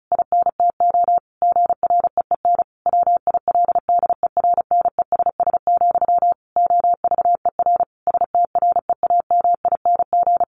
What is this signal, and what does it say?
Morse code, heard as a keyed tone, about 34 wpm, 700 Hz. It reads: INTO GREEN WILDERNESS, OVER STREAMING
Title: A Sound Of Thunder 35wpm